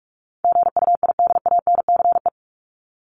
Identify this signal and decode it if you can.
Morse code, heard as a keyed tone, about 44 wpm, 700 Hz. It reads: GUIDANCE
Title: Top 1501-1750 Words 45wpm